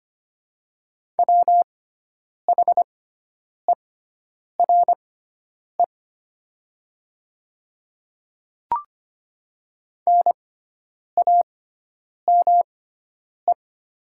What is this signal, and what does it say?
Morse code, heard as a keyed tone, about 25 wpm, spaced as for 9 wpm, 690 Hz. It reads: WHERE NAME
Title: 6x Character Spacing - Top 200 Words - 25wpm